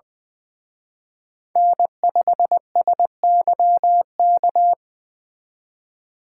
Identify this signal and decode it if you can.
Morse code, heard as a keyed tone, about 23 wpm, 700 Hz. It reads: N5SYK
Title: Speed Racing - USA 1x3 Call Signs 20wpm